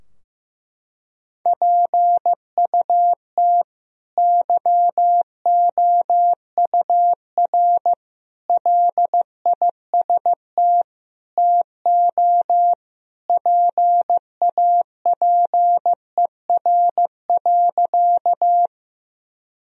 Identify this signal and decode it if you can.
Morse code, heard as a keyed tone, about 15 wpm, 700 Hz. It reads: PUT YOUR LIST TO PAPER.
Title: Sentences from Top 1000 Words - Review Part 2 - 15wpm